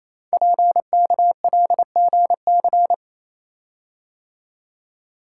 Morse code, heard as a keyed tone, about 28 wpm, 700 Hz. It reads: PKLGC
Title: Code Groups - 5 Letter:Numbers - 28wpm